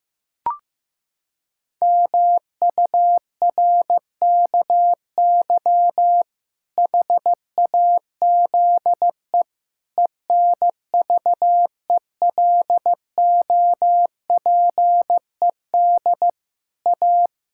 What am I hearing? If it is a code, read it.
Morse code, heard as a keyed tone, about 15 wpm, 700 Hz. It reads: MURKY HAZE ENVELOPED A
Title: Pangram Challenge v2 - 15wpm